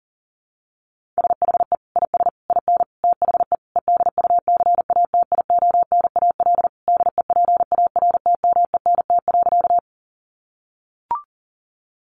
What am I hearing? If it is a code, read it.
Morse code, heard as a keyed tone, about 37 wpm, 700 Hz. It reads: SHE IS IN THE EDUCATIONAL DEPARTMENT.
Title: Sentences from Top 1200 Words 40wpm